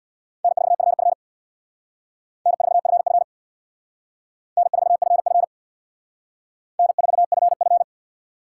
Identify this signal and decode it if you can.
Morse code, heard as a keyed tone, about 64 wpm, 700 Hz. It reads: N4FF N4FF N4FF N4FF